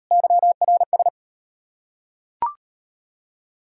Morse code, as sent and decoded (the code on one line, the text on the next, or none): -.-- .-. ...
YRS